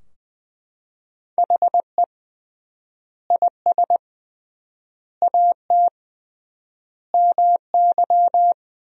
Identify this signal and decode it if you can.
Morse code, heard as a keyed tone, about 20 wpm, 700 Hz. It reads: HE IS AT MY
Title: 3x Word-Spacing - Sentences from Top 200 Words - 20wpm